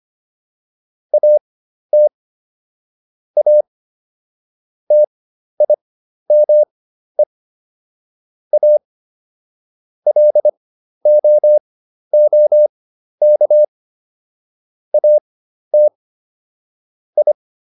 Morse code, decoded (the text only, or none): AT A TIME A LOOK AT I